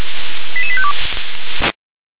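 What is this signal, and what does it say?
Five-tone selective call (ZVEI), heard as quick runs of five tones, tones of about 80 ms. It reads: 88852
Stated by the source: ZVEI five-tone selective call 88853 received on an SDR.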